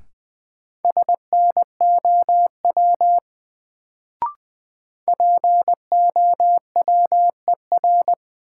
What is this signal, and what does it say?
Morse code, heard as a keyed tone, about 20 wpm, 700 Hz. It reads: SNOW POWER
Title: Sentences from Top 500 Words - New Word Drill - 20wpm